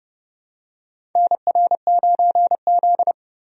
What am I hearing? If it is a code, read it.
Morse code, heard as a keyed tone, about 30 wpm, 700 Hz. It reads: NR9Z
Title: USA Call Signs - 4 Characters - 30wpm